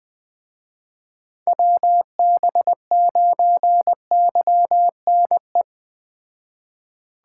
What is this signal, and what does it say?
Morse code, heard as a keyed tone, about 20 wpm, 700 Hz. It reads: WB9YNE
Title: Speed Racing - USA 2x3 Call Signs 17wpm